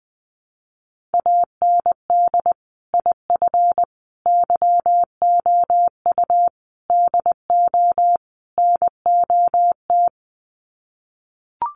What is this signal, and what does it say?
Morse code, heard as a keyed tone, about 20 wpm, 700 Hz. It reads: AND IF YOU DO NOT
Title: Sets of 5 Words 20wpm